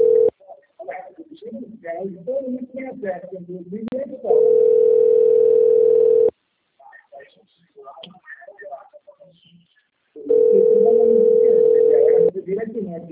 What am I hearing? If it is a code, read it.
Ringback tone (North America), 440 + 480 Hz, with speech.
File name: out- - - - - .17544